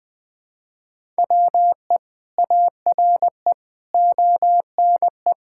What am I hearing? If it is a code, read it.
Morse code, heard as a keyed tone, about 20 wpm, 700 Hz. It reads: WE ARE ONE
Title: Sentences from Top 1000 Words - Review Part 6 - 20wpm